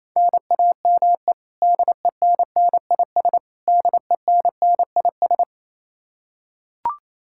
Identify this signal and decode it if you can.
Morse code, heard as a keyed tone, about 28 wpm, 700 Hz. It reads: NAME DENNIS DENNIS
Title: Practice QSO - Exchange 1 - Beginning - 28wpm